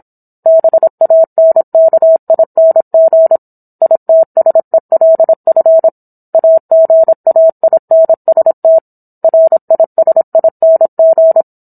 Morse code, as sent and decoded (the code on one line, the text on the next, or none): -... .- -. -.- .. -. --. / .. - ... .-.. ..-. / .- --. .- .. -. ... - / .-. .. ... .. -. --.
BANKING ITSELF AGAINST RISING